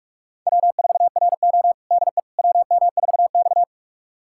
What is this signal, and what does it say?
Morse code, heard as a keyed tone, about 42 wpm, 700 Hz. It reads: W4RO DE WM4X
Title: USA Callsign DE Callsign - 4 Characters - 45wpm